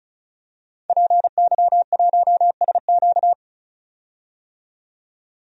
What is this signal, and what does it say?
Morse code, heard as a keyed tone, about 34 wpm, 700 Hz. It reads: PY1SQ
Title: DXCC-100 Call Signs - 5 Characters - 35wpm